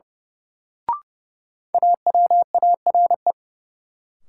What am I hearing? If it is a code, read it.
Morse code, heard as a keyed tone, about 32 wpm, 700 Hz. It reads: AWARE
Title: Sentences from Top 1200 Words - New Word Drill - 30wpm